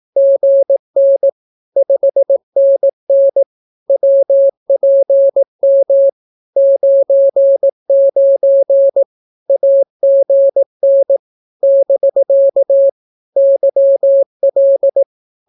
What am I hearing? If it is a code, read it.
Morse code, heard as a keyed tone, about 18 wpm, 550 Hz. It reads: GN 5NN WPM 99 AGN <BK> YL
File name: naukapodstaw / skroty / skroty